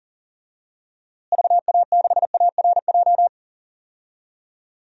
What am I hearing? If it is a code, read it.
Morse code, heard as a keyed tone, about 40 wpm, 700 Hz. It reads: VA6ARJ
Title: DXCC-100 Call Signs - 6 Characters - Repeated 2x - 40wpm